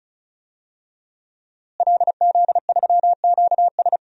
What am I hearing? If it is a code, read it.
Morse code, heard as a keyed tone, about 36 wpm, 700 Hz. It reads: LZ3QS
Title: DXCC-100 Call Signs - 5 Characters - Repeated 3x - 35wpm